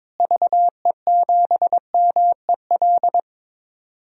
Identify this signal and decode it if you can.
Morse code, heard as a keyed tone, about 22 wpm, 700 Hz. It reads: VE7MEL
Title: DXCC-100 Call Signs - 6 Characters - Repeated 3x - 22wpm